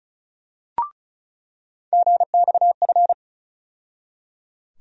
Morse code, heard as a keyed tone, about 35 wpm, 700 Hz. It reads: GXF
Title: USA Call Sign Postfixes - 3 Letters 35wpm